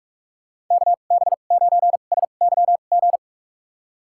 Morse code, heard as a keyed tone, about 43 wpm, spaced as for 32 wpm, 700 Hz. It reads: KB9SYG
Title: USA Call Signs - 2x Character Spacing - 45wpm